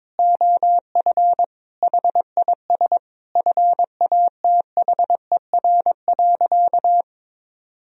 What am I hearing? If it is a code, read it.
Morse code, heard as a keyed tone, about 22 wpm, 700 Hz. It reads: OF HIS FATHER.
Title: Sentences from Top 1000 Words - Review Part 1 - 22wpm